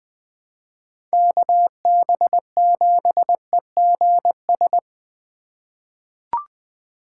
Morse code, heard as a keyed tone, about 20 wpm, 700 Hz. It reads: KB7EGS
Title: USA Call Signs - 2x Character Spacing - 20wpm